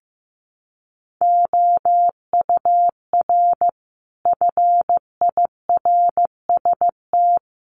Morse code, heard as a keyed tone, about 15 wpm, 700 Hz. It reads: OUR FIRST
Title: Sentences from Top 1300 Words 15wpm